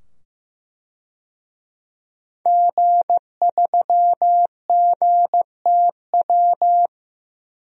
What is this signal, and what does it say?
Morse code, heard as a keyed tone, about 15 wpm, 700 Hz. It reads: G3GTW
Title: DXCC-100 Call Signs - 5 Characters - 15wpm